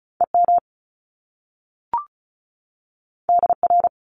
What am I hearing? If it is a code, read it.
Morse code, heard as a keyed tone, about 36 wpm, 700 Hz. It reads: EM DR
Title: Top Two Letter Combinations in Words 35wpm